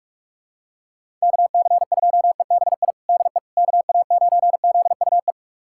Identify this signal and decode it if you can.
Morse code, heard as a keyed tone, about 44 wpm, 700 Hz. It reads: KC2EBI DE KA9ZUE
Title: USA Callsign DE Callsign 45wpm